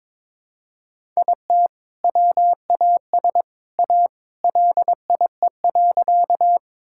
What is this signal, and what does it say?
Morse code, heard as a keyed tone, about 21 wpm, 700 Hz. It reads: IT WAS A LIE.
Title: Sentences from Top 1600 Words 22wpm